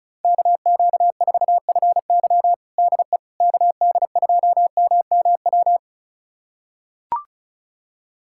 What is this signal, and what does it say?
Morse code, heard as a keyed tone, about 36 wpm, 700 Hz. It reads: KQ4FY DE KD2MMW
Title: USA Callsign DE Callsign 35wpm